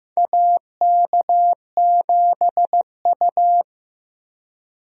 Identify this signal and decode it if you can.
Morse code, heard as a keyed tone, about 16 wpm, 700 Hz. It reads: AK7U
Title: USA Call Signs - 4 Characters - Repeated 2x - 15wpm